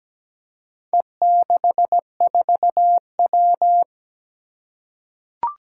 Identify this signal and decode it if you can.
Morse code, heard as a keyed tone, about 17 wpm, 700 Hz. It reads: E64W
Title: Four Challenging Letter-Number Combinations 17wpm